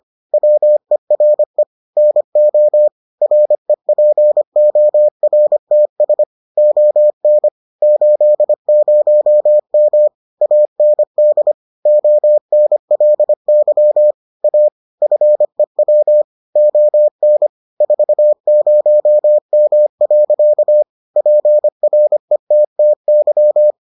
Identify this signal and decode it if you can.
Morse code, heard as a keyed tone, about 25 wpm, 600 Hz. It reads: WERE NO REPORTS ON 80M AND ONLY A FEW ON 40M. PRETTY